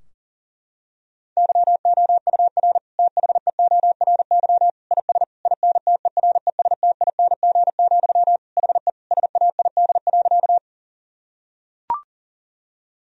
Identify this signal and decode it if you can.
Morse code, heard as a keyed tone, about 40 wpm, 700 Hz. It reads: YOUR THEORY IS INTERESTING, HE SAID.
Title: Sentences from Top 1000 Words - Review Part 3 - 40wpm